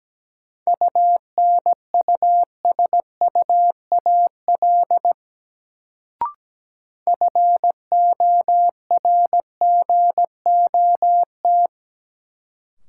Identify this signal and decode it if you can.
Morse code, heard as a keyed tone, about 17 wpm, 700 Hz. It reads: UNUSUAL FORGOT
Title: Sentences from Top 1400 Words - New Word Drill - 17wpm